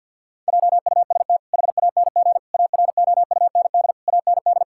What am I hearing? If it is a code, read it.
Morse code, heard as a keyed tone, about 50 wpm, 700 Hz. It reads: JUST HANG AROUND AND
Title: Sentences from Top 1600 Words 50wpm